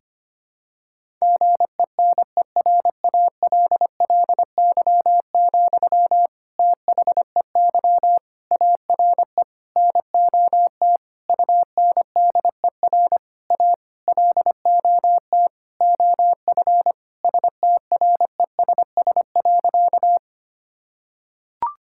Morse code, heard as a keyed tone, about 25 wpm, 700 Hz. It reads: GENERALLY, THEY ARE NOT UNDER A LOT OF STRESS.
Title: Sentences from Top 1000 Words - Review Part 6 - 25wpm